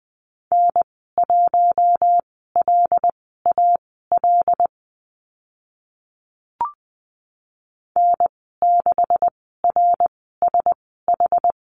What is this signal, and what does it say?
Morse code, heard as a keyed tone, about 20 wpm, spaced as for 15 wpm, 700 Hz. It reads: N1LAL N6RSH
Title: 140 Lesson 23g - Letter f - Call Signs 20wpm